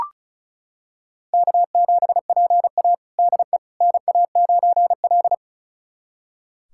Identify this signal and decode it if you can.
Morse code, heard as a keyed tone, about 36 wpm, 700 Hz. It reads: K7PA DE NA9L
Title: USA Callsign DE Callsign - 4 Characters - 35wpm